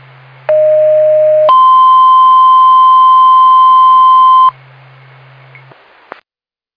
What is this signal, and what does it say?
Two-tone page, heard as a one-second tone, then a three-second tone, about 620 then 1040 Hz. It is heard from 0.5 to 4.5 s.